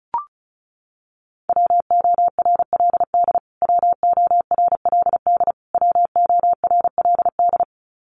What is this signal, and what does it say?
Morse code, heard as a keyed tone, about 34 wpm, 700 Hz. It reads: WORLD WORLD WORLD
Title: Top 500 Words Repeated 35wpm